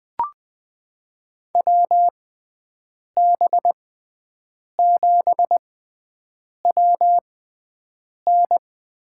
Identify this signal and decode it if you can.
Morse code, heard as a keyed tone, about 21 wpm, spaced as for 7 wpm, 700 Hz. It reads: WB7WN
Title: USA Call Signs - 6 Characters - 6x Character Spacing - 20wpm